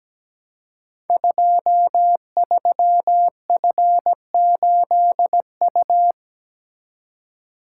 Morse code, heard as a keyed tone, about 17 wpm, 700 Hz. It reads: 23F8U
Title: Code Groups - 5 Letter:Numbers - 17wpm